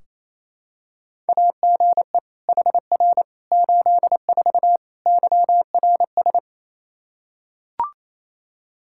Morse code, heard as a keyed tone, about 28 wpm, 700 Hz. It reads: AGE HR 84 YRS
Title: Practice QSO - Exchange 3 - More Info - 28wpm